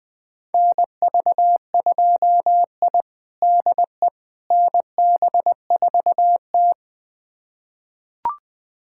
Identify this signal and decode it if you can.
Morse code, heard as a keyed tone, about 20 wpm, 700 Hz. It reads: NV2I DE NB4T
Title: USA Callsign DE Callsign - 4 Characters - 20wpm